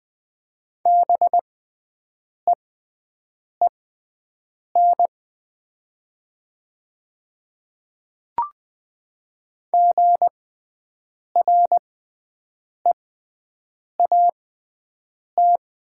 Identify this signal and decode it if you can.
Morse code, heard as a keyed tone, about 20 wpm, spaced as for 7 wpm, 700 Hz. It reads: BEEN GREAT
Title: 6x Character Spacing - Top 200 Words - 20wpm